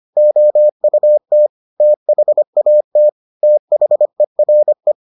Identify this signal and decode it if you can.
Morse code, heard as a keyed tone, about 25 wpm, 600 Hz. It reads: OUT THAT THERE